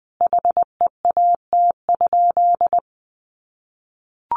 Morse code, heard as a keyed tone, about 20 wpm, 700 Hz. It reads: HEAT?
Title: Sentences from Top 1400 Words 20wpm